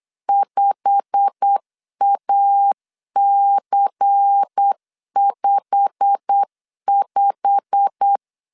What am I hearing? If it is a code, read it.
Morse code, heard as a keyed tone, about 8 wpm, 800 Hz. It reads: 5AC55
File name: LEZIONI / 8 YJ45 / 8 YJ45 8wpm 3214